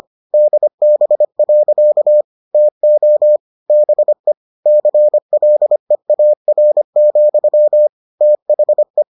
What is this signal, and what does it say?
Morse code, heard as a keyed tone, about 25 wpm, 600 Hz. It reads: DB. TO BE CLEAR, THE